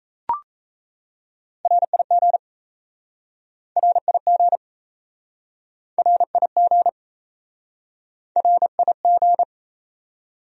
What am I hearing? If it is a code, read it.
Morse code, heard as a keyed tone, about 38 wpm, 700 Hz. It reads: RIG RIG RIG RIG